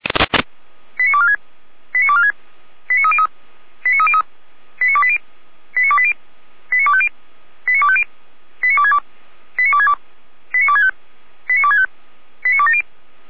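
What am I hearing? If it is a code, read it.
Five-tone selective call (ZVEI), heard as quick runs of five tones, tones of about 80 ms. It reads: 89246 89246 89303 89303 89280 89280 89250 89250 89262 89262 89265 89265 89270
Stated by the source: SDR-received ZVEI five-tone selective call.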